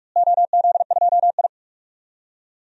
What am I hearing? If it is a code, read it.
Morse code, heard as a keyed tone, about 44 wpm, 700 Hz. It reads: OZ2I
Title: DXCC-100 Call Signs - 4 Characters - Repeated 3x - 45wpm